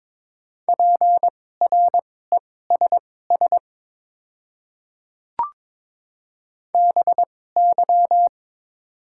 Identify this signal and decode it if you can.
Morse code, heard as a keyed tone, about 21 wpm, spaced as for 16 wpm, 700 Hz. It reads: PRESS BY
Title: 2x Character Spacing - Top 300 Words - 22wpm